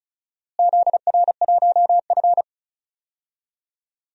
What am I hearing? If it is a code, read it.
Morse code, heard as a keyed tone, about 36 wpm, 700 Hz. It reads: ZR1F
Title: DXCC-100 Call Signs - 4 Characters - Repeated 2x - 35wpm